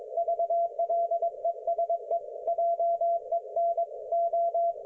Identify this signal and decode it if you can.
Morse code, heard as a keyed tone, about 21 wpm, 680 Hz, with radio noise. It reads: V LESE JEN O
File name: Archiv / CUC216 / ok1fhi cul216